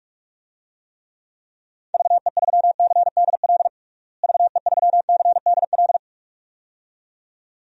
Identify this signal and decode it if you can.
Morse code, heard as a keyed tone, about 41 wpm, 700 Hz. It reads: VE3CDL VE3CDL
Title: DXCC-100 Call Signs - 6 Characters - Repeated 2x - 45wpm